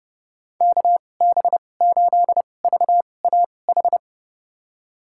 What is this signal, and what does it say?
Morse code, heard as a keyed tone, about 32 wpm, spaced as for 22 wpm, 700 Hz. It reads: KB8VAH
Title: USA Call Signs - 6 Characters - 2x Character Spacing - 30wpm